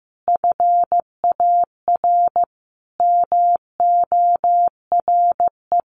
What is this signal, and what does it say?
Morse code, heard as a keyed tone, about 15 wpm, 700 Hz. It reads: FAR MORE
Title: Sentences from Top 1200 Words 15wpm